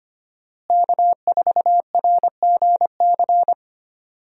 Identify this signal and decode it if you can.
Morse code, heard as a keyed tone, about 25 wpm, 700 Hz. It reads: K4RGC
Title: USA Call Signs - 5 Characters - Repeated 2x - 25wpm